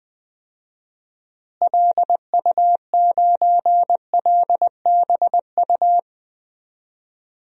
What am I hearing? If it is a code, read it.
Morse code, heard as a keyed tone, about 20 wpm, 700 Hz. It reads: LU9LBU
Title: DXCC-100 Call Signs - 6 Characters - Repeated 2x - 20wpm